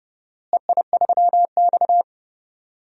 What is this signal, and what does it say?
Morse code, heard as a keyed tone, about 30 wpm, 700 Hz. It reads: EI3X
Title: DXCC-100 Call Signs - 4 Characters - Repeated 3x - 30wpm